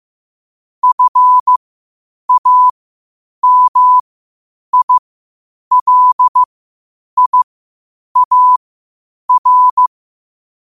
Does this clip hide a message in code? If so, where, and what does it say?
Yes: Morse code from 0.8 to 9.9 s, heard as a keyed tone, about 15 wpm, spaced as for 8 wpm, 1000 Hz. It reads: FAMILIAR